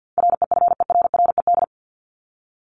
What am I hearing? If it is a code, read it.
Morse code, heard as a keyed tone, about 50 wpm, 700 Hz. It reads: REFERRED
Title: ED Suffix - Top 100 Words with Base Words 50wpm